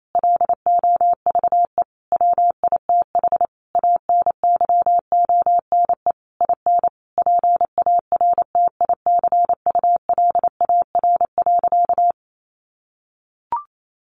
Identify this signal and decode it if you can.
Morse code, heard as a keyed tone, about 28 wpm, 700 Hz. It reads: LOVE WITH ANYONE IN PARTICULAR.
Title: Sentences from Top 900 Words 28wpm